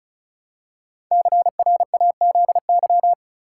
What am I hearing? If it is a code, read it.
Morse code, heard as a keyed tone, about 36 wpm, 700 Hz. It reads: CRAZY
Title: Five Letter Words - 35wpm